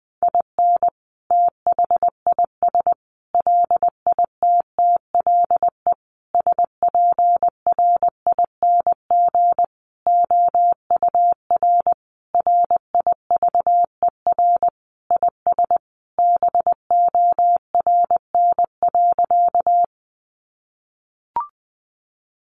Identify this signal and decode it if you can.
Morse code, heard as a keyed tone, about 20 wpm, 700 Hz. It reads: IN THIS LITTLE SPRING OUR RIVER IS BORN.
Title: Sentences from Top 1200 Words 20wpm